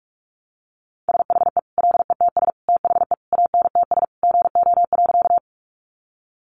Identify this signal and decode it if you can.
Morse code, heard as a keyed tone, about 44 wpm, 700 Hz. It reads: SHE LETS THE ANTS GO.
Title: Additional Easy Phrases 45wpm